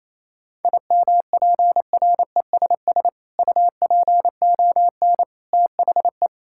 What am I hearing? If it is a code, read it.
Morse code, heard as a keyed tone, about 28 wpm, 700 Hz. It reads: IMPRESS UPON THE
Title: Sentences from Top 1600 Words 28wpm